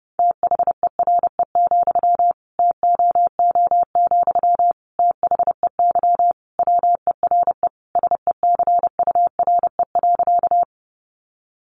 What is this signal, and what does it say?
Morse code, heard as a keyed tone, about 30 wpm, 700 Hz. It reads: THERE, TOO, THEY WERE SECURE.
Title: Sentences from Top 1200 Words 30wpm